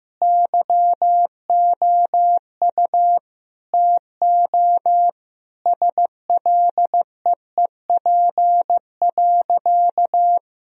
Morse code, heard as a keyed tone, about 15 wpm, 700 Hz. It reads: YOU TO SLEEP.
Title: Sentences from Top 1000 Words - Review Part 2 - 15wpm